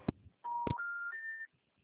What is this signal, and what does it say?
Special information tone.
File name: out- -2008- - - .95723